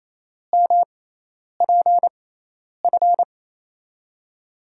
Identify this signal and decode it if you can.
Morse code, heard as a keyed tone, about 28 wpm, spaced as for 10 wpm, 700 Hz. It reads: MPF
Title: USA Call Signs - 6x Character Spacing - 28wpm